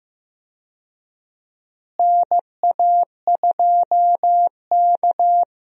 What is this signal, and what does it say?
Morse code, heard as a keyed tone, about 15 wpm, 700 Hz. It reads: NA2K
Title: USA Call Signs - 4 Characters - Repeated 2x - 15wpm